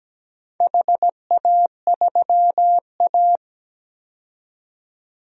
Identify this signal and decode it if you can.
Morse code, heard as a keyed tone, about 17 wpm, 700 Hz. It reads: HA3A
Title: DXCC-100 Call Signs - 4 Characters - 17wpm